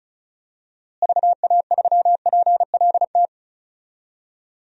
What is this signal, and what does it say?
Morse code, heard as a keyed tone, about 36 wpm, 700 Hz. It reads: VA3PLT
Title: DXCC-100 Call Signs - 6 Characters - Repeated 3x - 35wpm